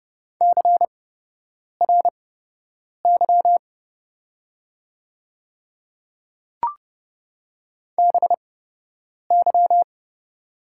Morse code, heard as a keyed tone, about 30 wpm, spaced as for 8 wpm, 700 Hz. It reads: CRY BY
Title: 8x Character Spacing - Top 500 Words - 30wpm